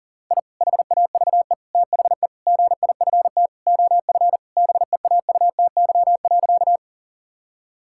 Morse code, heard as a keyed tone, about 40 wpm, 700 Hz. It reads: I HAVE THE GIFT OF BEAUTY.
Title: Sentences from Top 1000 Words - Review Part 6 - 40wpm